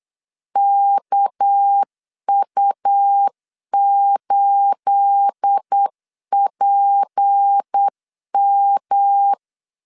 Morse code, heard as a keyed tone, about 8 wpm, 800 Hz. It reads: KU8PM
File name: LEZIONI / 9 28 8wpm 3512